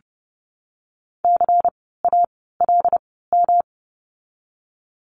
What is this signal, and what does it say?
Morse code, heard as a keyed tone, about 30 wpm, spaced as for 17 wpm, 700 Hz. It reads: CALM
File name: course / 30:10wpm / 108 Lesson 18f - Letter m - Words 30wpm